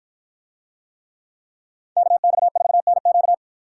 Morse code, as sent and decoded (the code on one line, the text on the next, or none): -.- -..- ....- -. --.-
KX4NQ